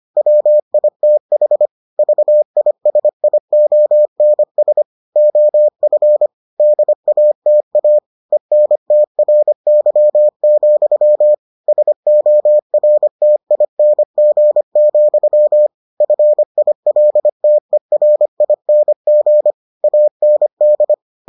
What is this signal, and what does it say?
Morse code, heard as a keyed tone, about 25 wpm, 600 Hz. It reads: WITH VISIONS OF DATA ENTRY, SORTING, FILTERING AND